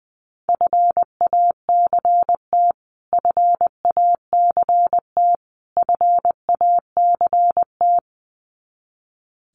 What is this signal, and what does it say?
Morse code, heard as a keyed tone, about 20 wpm, 700 Hz. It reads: FACT FACT FACT
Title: Top 500 Words Repeated 20wpm